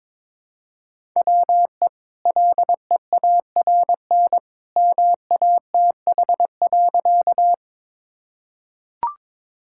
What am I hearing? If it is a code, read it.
Morse code, heard as a keyed tone, about 22 wpm, 700 Hz. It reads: WE LEARN MATH.
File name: practice / 22wpm / Easy Sentences with Hints - 22wpm